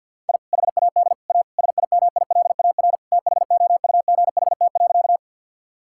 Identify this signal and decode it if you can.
Morse code, heard as a keyed tone, about 50 wpm, 700 Hz. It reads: I HAD A SIMILAR THOUGHT.
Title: Sentences from Top 1000 Words - Review Part 5 - 50wpm